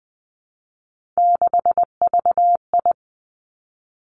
Morse code, heard as a keyed tone, about 20 wpm, 700 Hz. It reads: 6VI
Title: Three Challenging Letter-Number Combinations 20wpm